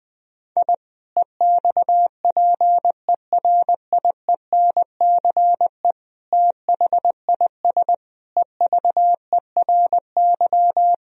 Morse code, heard as a keyed tone, about 20 wpm, 700 Hz. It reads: I EXPERIENCE THIS EVERY
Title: Sentences from Top 1000 Words - Review Part 3 - 20wpm